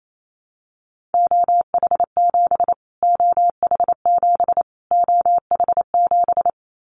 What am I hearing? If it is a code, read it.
Morse code, heard as a keyed tone, about 28 wpm, 700 Hz. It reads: OH7 OH7 OH7
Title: Callsign Prefixes for Top 25 DXCC - Repeated 3x - 28wpm